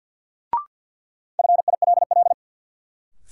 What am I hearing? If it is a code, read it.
Morse code, heard as a keyed tone, about 51 wpm, 700 Hz. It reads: FILL
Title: Sentences from Top 1000 Words - Review Part 3 - New Word Drill - 50wpm